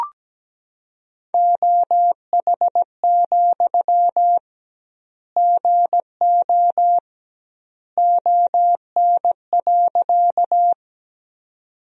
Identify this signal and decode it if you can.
Morse code, heard as a keyed tone, about 17 wpm, 700 Hz. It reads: OH, GO ON.
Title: 2x Word-Spacing - Sentences from Top 500 Words - 17wpm